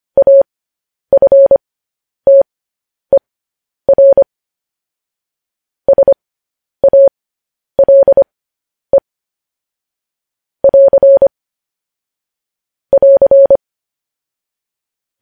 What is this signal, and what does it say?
Morse code, heard as a keyed tone, about 25 wpm, spaced as for 10 wpm, 550 Hz. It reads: AFTER SALE <AR> <AR>
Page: news-headlines-in-morse-code-at-10-wpm